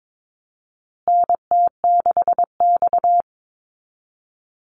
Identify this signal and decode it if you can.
Morse code, heard as a keyed tone, about 22 wpm, 700 Hz. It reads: NT6X
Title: Call Signs Repeated 2x - 22wpm